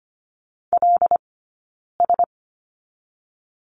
Morse code, heard as a keyed tone, about 25 wpm, 700 Hz. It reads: L S
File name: course / 25:10wpm / 092 Lesson 16d - Letter c - Two Characters 25wpm